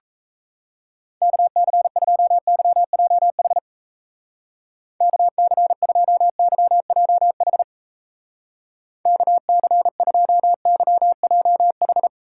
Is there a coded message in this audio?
Yes: Morse code from 1.2 to 12.1 s, heard as a keyed tone, about 35 wpm, 700 Hz. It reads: KC2YJH KC2YJH KC2YJH